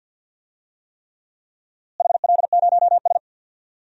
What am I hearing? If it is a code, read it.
Morse code, heard as a keyed tone, about 50 wpm, 700 Hz. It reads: HL0S
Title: DXCC-100 Call Signs - 4 Characters - Repeated 3x - 50wpm